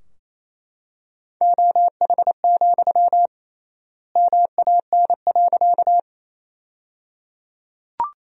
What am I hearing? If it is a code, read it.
Morse code, heard as a keyed tone, about 27 wpm, 700 Hz. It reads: OH, MAN.
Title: 3x Word-Spacing - Sentences from Top 500 Words - 28wpm